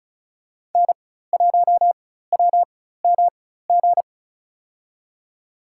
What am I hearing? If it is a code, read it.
Morse code, heard as a keyed tone, about 34 wpm, spaced as for 16 wpm, 700 Hz. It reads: N1WMG
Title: USA Call Signs - 5 Characters - 4x Character Spacing - 35wpm